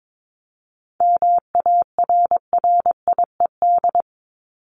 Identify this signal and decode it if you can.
Morse code, heard as a keyed tone, about 22 wpm, 700 Hz. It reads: MARRIED
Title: ED Suffix - Top 100 Words with Base Words 22wpm